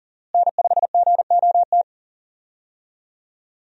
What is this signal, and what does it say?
Morse code, heard as a keyed tone, about 40 wpm, 700 Hz. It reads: N5GOT
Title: USA Call Signs - 5 Characters - Repeated 2x - 40wpm